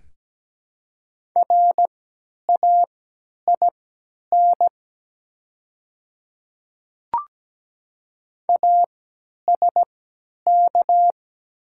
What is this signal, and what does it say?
Morse code, heard as a keyed tone, about 17 wpm, spaced as for 10 wpm, 700 Hz. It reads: RAIN ASK
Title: 3x Character Spacing - Top 300 Words - 17wpm